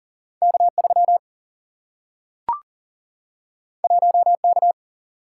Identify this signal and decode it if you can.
Morse code, heard as a keyed tone, about 40 wpm, 700 Hz. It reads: K3 1K
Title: Single-Letter Single-Number 40wpm